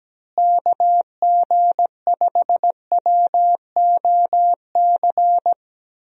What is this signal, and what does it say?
Morse code, heard as a keyed tone, about 17 wpm, 700 Hz. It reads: KG5WOC
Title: USA Call Signs - 6 Characters - Repeated 3x - 17wpm